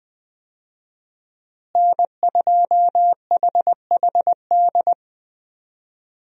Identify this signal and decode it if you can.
Morse code, heard as a keyed tone, about 20 wpm, 700 Hz. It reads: N2HHD
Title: USA Call Signs - 5 Characters - Repeated 2x - 20wpm